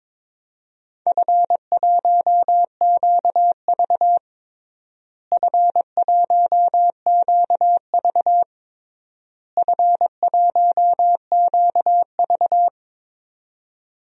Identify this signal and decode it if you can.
Morse code, heard as a keyed tone, about 22 wpm, 700 Hz. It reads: F1QV F1QV F1QV
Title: DXCC-100 Call Signs - 4 Characters - Repeated 3x - 22wpm